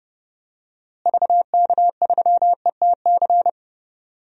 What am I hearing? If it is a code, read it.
Morse code, heard as a keyed tone, about 32 wpm, 700 Hz. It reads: VK3ETC
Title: DXCC-100 Call Signs - 6 Characters - Repeated 3x - 30wpm